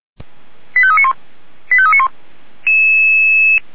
Five-tone selective call (ZVEI), heard as quick runs of five tones, tones of about 70 ms. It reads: 85391 85391